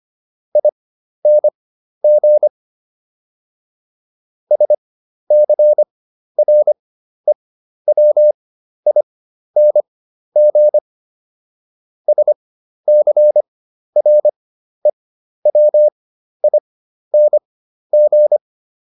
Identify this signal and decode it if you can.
Morse code, heard as a keyed tone, about 25 wpm, spaced as for 12 wpm, 600 Hz. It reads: ING SCREWING SCREWING